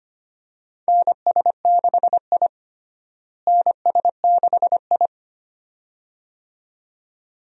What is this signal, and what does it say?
Morse code, heard as a keyed tone, about 25 wpm, 700 Hz. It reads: NS6I NS6I
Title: USA Call Signs - 4 Characters - Repeated 2x - 25wpm